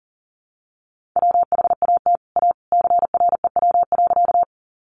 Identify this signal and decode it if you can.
Morse code, heard as a keyed tone, about 40 wpm, 700 Hz. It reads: WHAT A CREW.
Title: Sentences from Top 1500 Words 40wpm